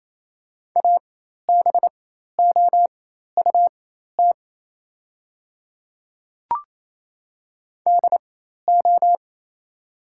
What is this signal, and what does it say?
Morse code, heard as a keyed tone, about 28 wpm, spaced as for 13 wpm, 690 Hz. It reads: ABOUT DO
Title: 4x Character Spacing - Top 100 Words - 28wpm